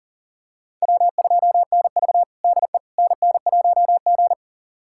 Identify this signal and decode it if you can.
Morse code, heard as a keyed tone, about 37 wpm, 700 Hz. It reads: W2NV DE NN1G
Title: USA Callsign DE Callsign - 4 Characters - 40wpm